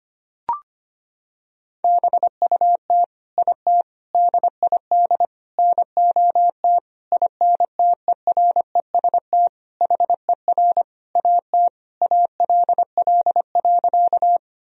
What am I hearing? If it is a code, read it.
Morse code, heard as a keyed tone, about 25 wpm, 700 Hz. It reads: BUT IT DID NOT INTEREST HER AT ALL.
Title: Sentences from Top 1000 Words - Review Part 2 - 25wpm